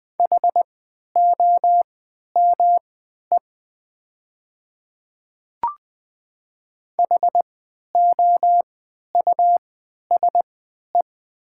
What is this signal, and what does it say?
Morse code, heard as a keyed tone, about 20 wpm, spaced as for 11 wpm, 690 Hz. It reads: HOME HOUSE
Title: 3x Character Spacing - Top 200 Words - 20wpm